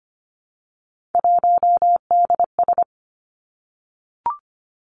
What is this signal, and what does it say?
Morse code, heard as a keyed tone, about 25 wpm, 700 Hz. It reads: 1DS
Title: Three Challenging Letter-Number Combinations 25wpm